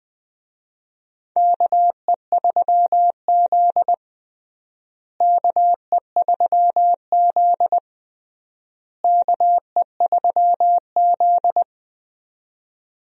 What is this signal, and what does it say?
Morse code, heard as a keyed tone, about 20 wpm, 700 Hz. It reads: KE3Z KE3Z KE3Z
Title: USA Call Signs - 4 Characters - Repeated 3x - 20wpm